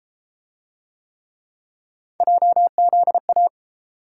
Morse code, heard as a keyed tone, about 31 wpm, 700 Hz. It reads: JZA